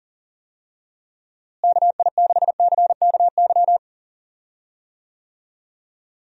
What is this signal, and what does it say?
Morse code, heard as a keyed tone, about 37 wpm, 700 Hz. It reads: KI6CKY
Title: USA Call Signs - 6 Characters - Repeated 3x - 40wpm